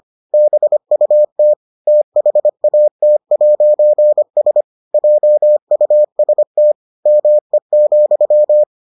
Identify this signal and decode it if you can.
Morse code, heard as a keyed tone, about 25 wpm, 600 Hz. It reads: BUT THAT'S JUST ME,